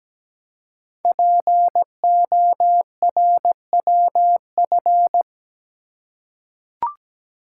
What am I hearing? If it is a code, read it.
Morse code, heard as a keyed tone, about 17 wpm, 700 Hz. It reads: PORWF
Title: Code Groups - 5 Letters - 17wpm